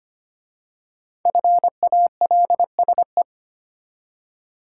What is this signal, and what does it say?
Morse code, heard as a keyed tone, about 25 wpm, 700 Hz. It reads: FALSE
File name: 5 Letter Words - First 3 Letters Given - 25wpm